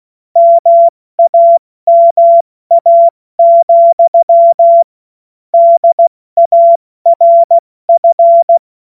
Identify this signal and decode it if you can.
Morse code, heard as a keyed tone, about 16 wpm, 680 Hz. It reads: MAMA, DARF
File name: Morsekurs / Witze / Witz 21. September 2021, 70 F80